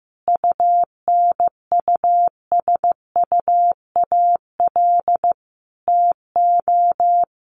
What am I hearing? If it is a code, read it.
Morse code, heard as a keyed tone, about 15 wpm, 700 Hz. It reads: UNUSUAL TO
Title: Sentences from Top 1400 Words 15wpm